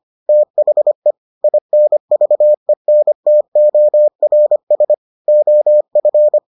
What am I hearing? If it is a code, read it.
Morse code, heard as a keyed tone, about 25 wpm, 600 Hz. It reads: THE INVENTORS OF